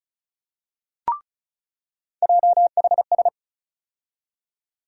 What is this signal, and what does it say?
Morse code, heard as a keyed tone, about 34 wpm, 700 Hz. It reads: JHS
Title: USA Call Sign Postfixes - 3 Letters 35wpm